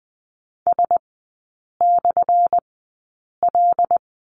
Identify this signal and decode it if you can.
Morse code, heard as a keyed tone, about 20 wpm, spaced as for 8 wpm, 700 Hz. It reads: S/L
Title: 178 Lesson 29e - Slash - Three Characters 20wpm